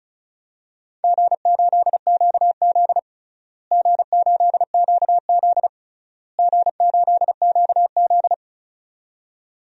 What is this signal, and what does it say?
Morse code, heard as a keyed tone, about 34 wpm, 700 Hz. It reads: G8QZ G8QZ G8QZ
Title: DXCC-100 Call Signs - 4 Characters - Repeated 3x - 35wpm